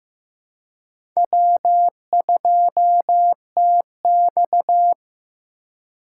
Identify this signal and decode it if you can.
Morse code, heard as a keyed tone, about 15 wpm, 700 Hz. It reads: W2TX
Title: USA Call Signs - 4 Characters - 3x Character Spacing - 15wpm